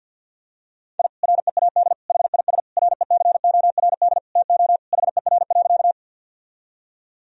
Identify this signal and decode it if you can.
Morse code, heard as a keyed tone, about 52 wpm, 700 Hz. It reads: I READ HIS RECORD TO HER.
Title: Sentences from Top 1000 Words - Review Part 3 - 50wpm